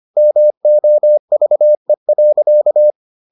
Morse code, heard as a keyed tone, about 25 wpm, 600 Hz. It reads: MOVE.